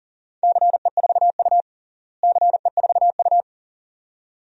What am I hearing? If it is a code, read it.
Morse code, heard as a keyed tone, about 37 wpm, 700 Hz. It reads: CE4U CE4U
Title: DXCC-100 Call Signs - 4 Characters - Repeated 2x - 40wpm